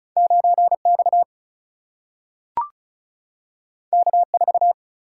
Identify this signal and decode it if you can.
Morse code, heard as a keyed tone, about 36 wpm, 700 Hz. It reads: 9X K4
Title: Single-Letter Single-Number 35wpm